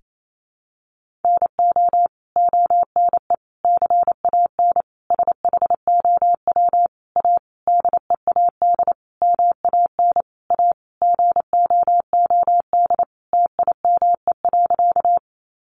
Morse code, heard as a keyed tone, about 28 wpm, 700 Hz. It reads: NO ONE CAN SHOW A DEAD MAN A GOOD TIME.
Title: Sentences from Top 1500 Words 28wpm